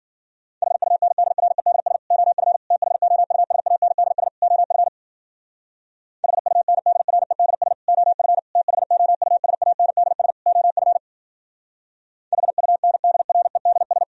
Morse code, heard as a keyed tone, about 52 wpm, 700 Hz. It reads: HUNDREDS OF THOUSANDS OF HUNDREDS OF THOUSANDS OF HUNDREDS